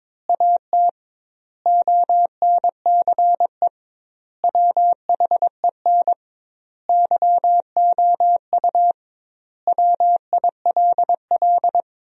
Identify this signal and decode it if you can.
Morse code, heard as a keyed tone, about 22 wpm, 700 Hz. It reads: AT ONCE WHEN YOU WILL
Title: 2x Word-Spacing - Sentences from Top 300 Words - 22wpm